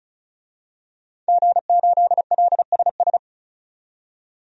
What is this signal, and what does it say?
Morse code, heard as a keyed tone, about 36 wpm, 700 Hz. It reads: G8LSS
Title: DXCC-100 Call Signs - 5 Characters - 35wpm